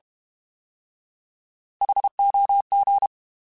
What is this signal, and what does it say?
Morse code, heard as a keyed tone, about 32 wpm, 800 Hz. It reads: HOG